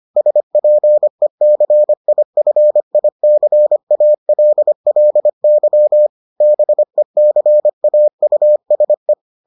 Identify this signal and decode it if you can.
Morse code, heard as a keyed tone, about 25 wpm, 600 Hz. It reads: SPECIFICALLY BECAUSE